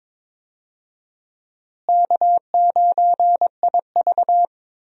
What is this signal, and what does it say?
Morse code, heard as a keyed tone, about 21 wpm, 700 Hz. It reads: K9IV
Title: USA Call Signs - 4 Characters - Repeated 3x - 22wpm